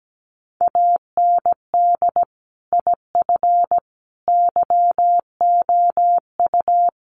Morse code, heard as a keyed tone, about 17 wpm, 700 Hz. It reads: AND IF YOU
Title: Sets of 5 Words Encore 17wpm